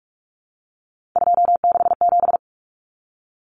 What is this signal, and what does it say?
Morse code, heard as a keyed tone, about 44 wpm, 700 Hz. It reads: 267
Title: Common Cut Numbers - 3-Digit 45wpm